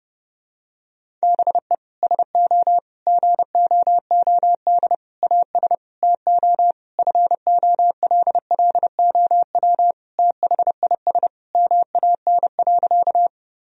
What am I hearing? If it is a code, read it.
Morse code, heard as a keyed tone, about 30 wpm, 700 Hz. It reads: BE SO GOOD AS TO FOLLOW THIS MAN.
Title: Sentences from Top 1000 Words - Review Part 1 - 30wpm